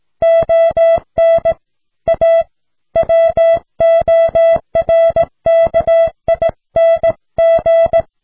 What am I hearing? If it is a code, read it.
Morse code, heard as a keyed tone, about 17 wpm, 660 Hz. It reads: ON A WORKING